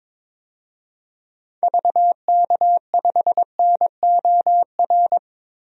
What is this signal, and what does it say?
Morse code, heard as a keyed tone, about 22 wpm, 700 Hz. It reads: VK5NOR
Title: DXCC-100 Call Signs - 6 Characters - Repeated 3x - 22wpm